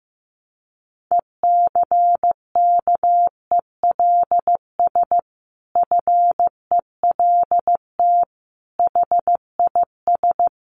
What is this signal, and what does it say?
Morse code, heard as a keyed tone, about 15 wpm, 700 Hz. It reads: ECKELS FELT HIS
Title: A Sound Of Thunder 15wpm